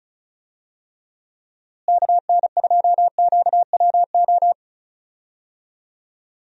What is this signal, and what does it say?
Morse code, heard as a keyed tone, about 36 wpm, 700 Hz. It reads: KN2QWO
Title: USA Call Signs - 6 Characters - Repeated 2x - 35wpm